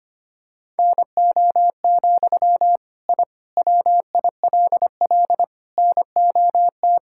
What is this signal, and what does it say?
Morse code, heard as a keyed tone, about 25 wpm, 700 Hz. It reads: NO, I WILL NOT
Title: Sentences from Top 1000 Words - Review Part 5 - 25wpm